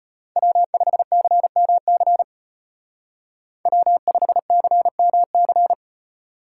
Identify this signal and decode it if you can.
Morse code, heard as a keyed tone, about 30 wpm, 700 Hz. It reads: W5CMC W5CMC
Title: Speed Racing - USA Call Signs 25wpm